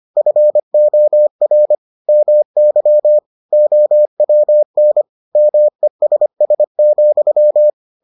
Morse code, heard as a keyed tone, about 25 wpm, 600 Hz. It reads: FOR MY OWN MESS,